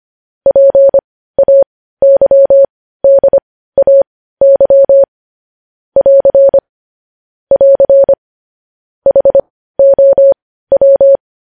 Morse code, heard as a keyed tone, about 25 wpm, spaced as for 15 wpm, 550 Hz. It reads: PAYDAY <AR> <AR> HOW
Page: news-headlines-in-morse-code-at-15-wpm